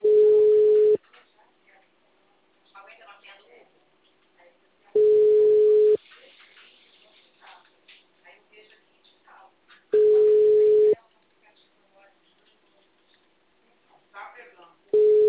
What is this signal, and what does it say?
Ringback tone (Europe, 425 Hz), with speech.
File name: out- -2009- - - .4